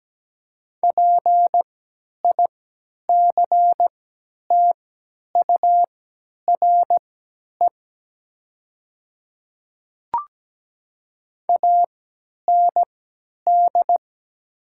Morse code, heard as a keyed tone, about 17 wpm, spaced as for 10 wpm, 700 Hz. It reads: PICTURE AND
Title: 3x Character Spacing - Top 300 Words - 17wpm